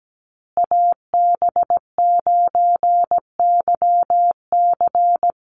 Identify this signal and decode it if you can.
Morse code, heard as a keyed tone, about 17 wpm, 700 Hz. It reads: AB9YC
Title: Call Signs Repeated 2x - 17wpm